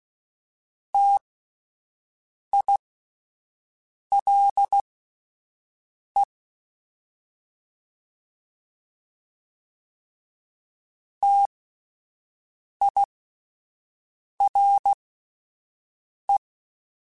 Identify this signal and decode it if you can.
Morse code, heard as a keyed tone, about 16 wpm, spaced as for 5 wpm, 800 Hz. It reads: TILE TIRE